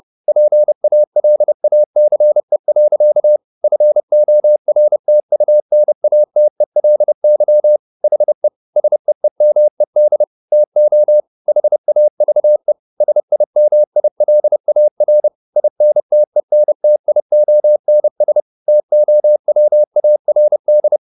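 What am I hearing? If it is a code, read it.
Morse code, heard as a keyed tone, about 30 wpm, 600 Hz. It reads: PALACE. FORTUNATELY HE SEEMED TO HAVE SIMILAR INTENTIONS TOWARD
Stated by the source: Morse code at 30 wpm, the novel Never by Ken Follett in English, from CW med Gnister.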